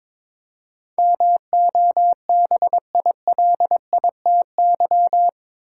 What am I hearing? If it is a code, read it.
Morse code, heard as a keyed tone, about 21 wpm, 700 Hz. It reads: MOBILITY
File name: practice / 22wpm / Eight Letter Words - 22wpm